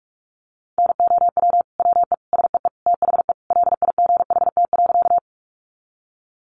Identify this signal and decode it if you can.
Morse code, heard as a keyed tone, about 44 wpm, 700 Hz. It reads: NOW WE SEE THE LIGHT.
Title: Sentences from Top 200 Words 45wpm